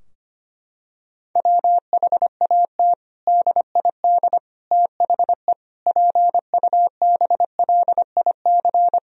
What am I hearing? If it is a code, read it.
Morse code, heard as a keyed tone, about 25 wpm, 700 Hz. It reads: WHAT DID THE PUBLIC
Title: Sentences from Top 1000 Words - Review Part 3 - 25wpm